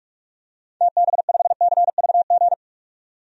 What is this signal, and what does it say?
Morse code, heard as a keyed tone, about 43 wpm, 700 Hz. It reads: TB5CVG
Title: DXCC-100 Call Signs - 6 Characters - 45wpm